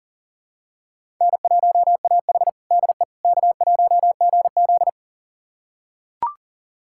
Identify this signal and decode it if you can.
Morse code, heard as a keyed tone, about 37 wpm, 700 Hz. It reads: N1AH DE K1GZ
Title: USA Callsign DE Callsign - 4 Characters - 40wpm